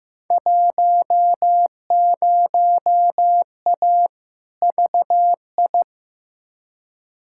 Morse code, heard as a keyed tone, about 15 wpm, 700 Hz. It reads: 10A VI
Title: Field Day Exchanges Encore - 15wpm